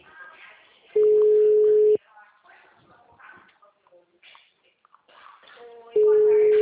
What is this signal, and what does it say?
Ringback tone (Europe, 425 Hz), with speech.